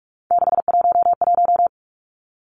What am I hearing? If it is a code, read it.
Morse code, heard as a keyed tone, about 46 wpm, 700 Hz. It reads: 611
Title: Three Challenging Letter-Number Combinations 45wpm